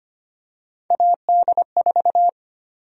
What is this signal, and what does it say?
Morse code, heard as a keyed tone, about 25 wpm, 700 Hz. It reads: AD4
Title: USA Call Sign Prefixes - 25wpm